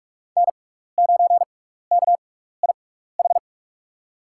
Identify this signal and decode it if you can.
Morse code, heard as a keyed tone, about 43 wpm, spaced as for 15 wpm, 700 Hz. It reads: N9KIH
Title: USA Call Signs - 5 Characters - 6x Character Spacing - 45wpm